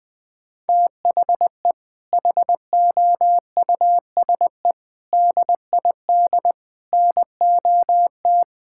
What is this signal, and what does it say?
Morse code, heard as a keyed tone, about 20 wpm, 700 Hz. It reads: THE HOUSE DID NOT
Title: Sentences from Top 1600 Words 20wpm